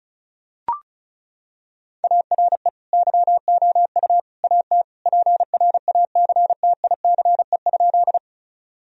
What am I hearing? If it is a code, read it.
Morse code, heard as a keyed tone, about 36 wpm, 700 Hz. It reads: ARE YOU AT PRACTICE?
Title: Sentences from Top 1000 Words - Review Part 4 - 35wpm